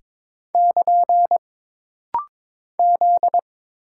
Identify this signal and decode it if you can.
Morse code, heard as a keyed tone, about 22 wpm, 700 Hz. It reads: <KN> Z
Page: This rapid-fire timing is geared to push your ICR (Instant Character Recognition) skills.